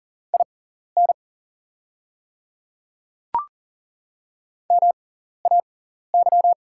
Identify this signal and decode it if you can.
Morse code, heard as a keyed tone, about 40 wpm, spaced as for 14 wpm, 690 Hz. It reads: IN MAY
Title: 6x Character Spacing - Top 100 Words - 40wpm